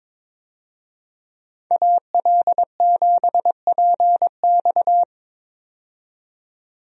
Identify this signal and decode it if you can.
Morse code, heard as a keyed tone, about 22 wpm, 700 Hz. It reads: AL7PX
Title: USA Call Signs - 5 Characters - 22wpm